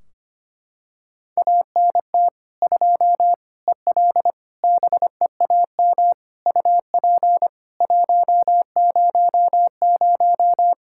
Morse code, heard as a keyed tone, about 25 wpm, 700 Hz. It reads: ANT 2 EL BEAM UP 100
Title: Practice QSO - Exchange 2 - Info - 25wpm